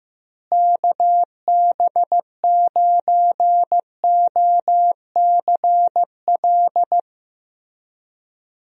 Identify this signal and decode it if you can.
Morse code, heard as a keyed tone, about 15 wpm, 700 Hz. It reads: KB9OCL
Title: USA Call Signs - 6 Characters - Repeated 3x - 15wpm